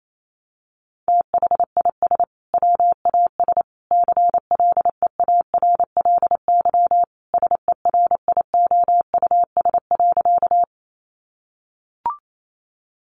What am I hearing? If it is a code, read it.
Morse code, heard as a keyed tone, about 28 wpm, 700 Hz. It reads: THIS WAS CLEARLY SERIOUS.
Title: Sentences from Top 1300 Words 28wpm